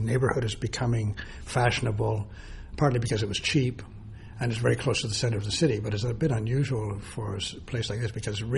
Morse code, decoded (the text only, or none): none